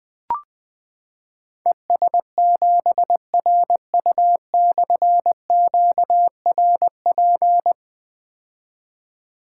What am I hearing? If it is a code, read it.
Morse code, heard as a keyed tone, about 20 wpm, 700 Hz. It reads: ES7RU/QRP
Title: Difficult Call Signs 20wpm